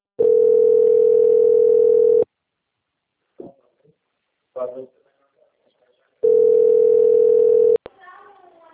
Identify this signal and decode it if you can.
Ringback tone (North America), 440 + 480 Hz, with speech.
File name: out- - - - - .3994